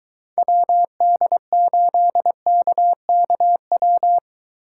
Morse code, heard as a keyed tone, about 23 wpm, 700 Hz. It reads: WD8KKW